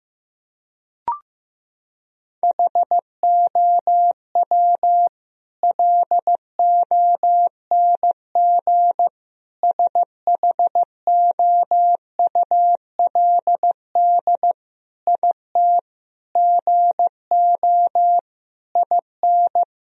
Morse code, heard as a keyed tone, about 15 wpm, 700 Hz. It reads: HOW LONG SHOULD IT GO IN
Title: Sentences from Top 1000 Words - Review Part 4 - 15wpm